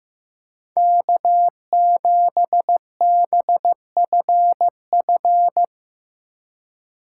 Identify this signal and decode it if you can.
Morse code, heard as a keyed tone, about 15 wpm, 700 Hz. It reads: K7BFF
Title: USA Call Signs - 5 Characters - 2x Character Spacing - 15wpm